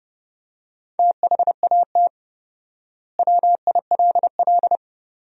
Morse code, heard as a keyed tone, about 30 wpm, 700 Hz. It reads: THAT WILL
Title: 4x Word-Spacing - Sentences from Top 100 Words - 30wpm